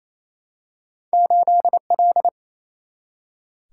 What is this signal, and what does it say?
Morse code, heard as a keyed tone, about 27 wpm, 700 Hz. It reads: 8 L
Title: Single-Letter Single-Number 28wpm